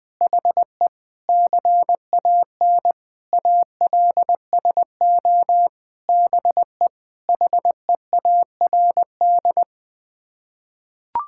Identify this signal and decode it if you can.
Morse code, heard as a keyed tone, about 20 wpm, 700 Hz. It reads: HE CAN ALSO BE HEARD
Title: Sets of 9 Words - 20wpm